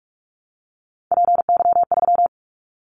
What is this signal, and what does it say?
Morse code, heard as a keyed tone, about 46 wpm, 700 Hz. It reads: PY3
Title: Callsign Prefixes for Top 100 DXCC - 45wpm